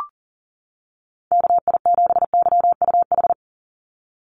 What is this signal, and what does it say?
Morse code, heard as a keyed tone, about 38 wpm, 700 Hz. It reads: KI7YUH
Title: USA Call Signs 40wpm